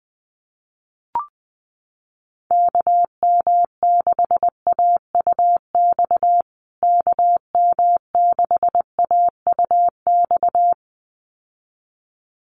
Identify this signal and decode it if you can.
Morse code, heard as a keyed tone, about 19 wpm, 700 Hz. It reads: KM6AUX KM6AUX
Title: USA Call Signs Repeated 2x 20wpm